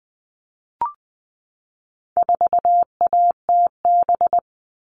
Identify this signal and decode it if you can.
Morse code, heard as a keyed tone, about 20 wpm, 700 Hz. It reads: 4ATB
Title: All Cut Numbers - 4-Digit 20wpm